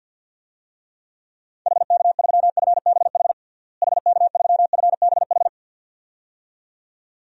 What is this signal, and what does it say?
Morse code, heard as a keyed tone, about 53 wpm, 700 Hz. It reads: HK3FBH HK3FBH
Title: DXCC-100 Call Signs - 6 Characters - Repeated 2x - 50wpm